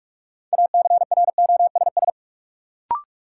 Morse code, heard as a keyed tone, about 43 wpm, 700 Hz. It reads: ACROSS
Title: Sentences from Top 700 Words - New Word Drill - 45wpm